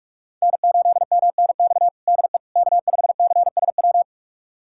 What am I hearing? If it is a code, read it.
Morse code, heard as a keyed tone, about 46 wpm, 700 Hz. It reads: N8MNX DE K5CSW
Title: USA Callsign DE Callsign 45wpm